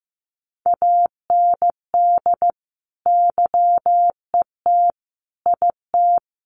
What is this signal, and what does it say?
Morse code, heard as a keyed tone, about 15 wpm, 700 Hz. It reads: AND YET IT
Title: Sentences from Top 800 Words 15wpm